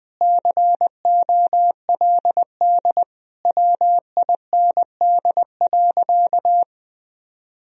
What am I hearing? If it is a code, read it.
Morse code, heard as a keyed tone, about 20 wpm, 700 Hz. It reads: COLD WIND.
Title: Sentences from Top 1000 Words - Review Part 2 - 20wpm